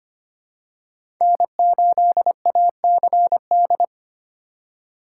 Morse code, heard as a keyed tone, about 25 wpm, 700 Hz. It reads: N8ACD
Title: USA Call Signs - 5 Characters - 3x Character Spacing - 25wpm